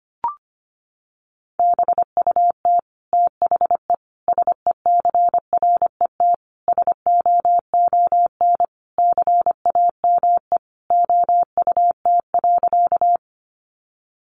Morse code, heard as a keyed tone, about 25 wpm, 700 Hz. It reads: BUT THE SECRET SOON CAME OUT.
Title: Sentences from Top 1200 Words 25wpm